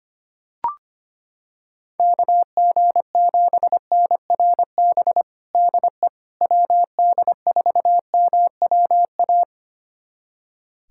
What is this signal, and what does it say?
Morse code, heard as a keyed tone, about 25 wpm, 700 Hz. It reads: KG7NRB DE WD4MWA
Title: USA Callsign DE Callsign 25wpm